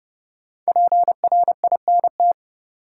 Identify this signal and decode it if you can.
Morse code, heard as a keyed tone, about 32 wpm, 700 Hz. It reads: PRINT
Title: Sentences from Top 1400 Words - New Word Drill Alt - 30wpm